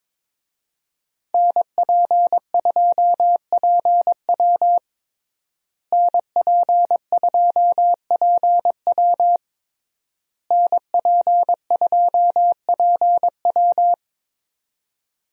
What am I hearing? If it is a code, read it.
Morse code, heard as a keyed tone, about 22 wpm, 700 Hz. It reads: NP2PW NP2PW NP2PW
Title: USA Call Signs - 5 Characters - Repeated 3x - 22wpm